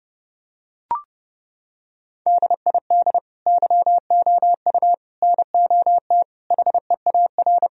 Morse code, heard as a keyed tone, about 30 wpm, 700 Hz. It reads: DID YOU NOT HEAR
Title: Sentences from Top 1000 Words - Review Part 2 - 30wpm